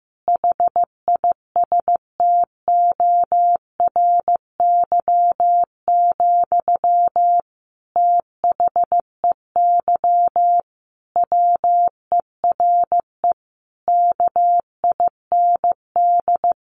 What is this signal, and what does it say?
Morse code, heard as a keyed tone, about 15 wpm, 700 Hz. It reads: HISTORY, THEY WERE KIND
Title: Sentences from Top 800 Words 15wpm